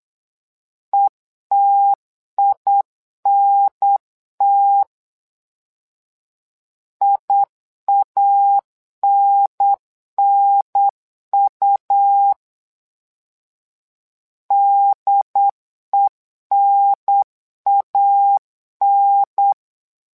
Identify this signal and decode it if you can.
Morse code, heard as a keyed tone, about 8 wpm, 800 Hz. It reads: ETINT IANNU DENAN
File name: LEZIONI / 2 UHDN / 2 UHDN 8wpm2232